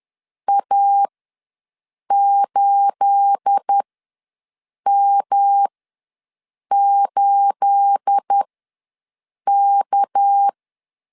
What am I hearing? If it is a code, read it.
Morse code, heard as a keyed tone, about 10 wpm, spaced as for 6 wpm, 800 Hz. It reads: A8M8K
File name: LEZIONI / 9 28 ws 10wpm 4124